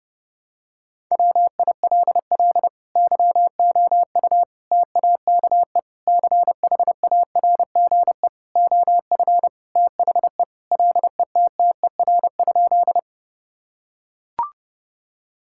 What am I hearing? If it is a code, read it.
Morse code, heard as a keyed tone, about 30 wpm, 700 Hz. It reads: WILL YOU TAKE CHARGE OF THE LETTER?
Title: Sentences from Top 1000 Words - Review Part 6 - 30wpm